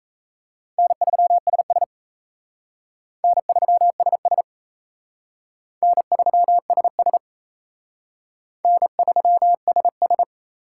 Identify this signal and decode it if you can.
Morse code, heard as a keyed tone, about 37 wpm, 700 Hz. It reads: N3SS N3SS N3SS N3SS